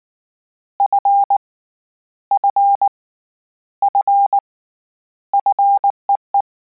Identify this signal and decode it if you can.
Morse code, heard as a keyed tone, about 19 wpm, 800 Hz. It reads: F F F FEE